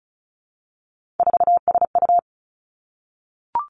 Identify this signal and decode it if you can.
Morse code, heard as a keyed tone, about 36 wpm, 700 Hz. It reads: 4SU
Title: Three Challenging Letter-Number Combinations 35wpm